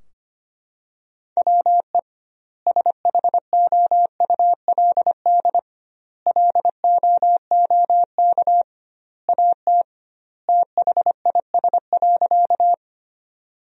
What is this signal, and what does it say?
Morse code, heard as a keyed tone, about 25 wpm, 700 Hz. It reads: WE SHOULD LOOK AT THIS.
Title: 2x Word-Spacing - Sentences from Top 300 Words - 25wpm